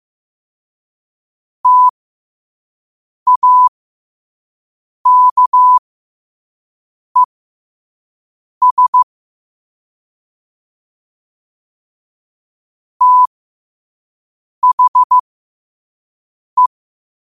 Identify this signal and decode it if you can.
Morse code, heard as a keyed tone, about 15 wpm, spaced as for 5 wpm, 1000 Hz. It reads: TAKES THE
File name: 5wpm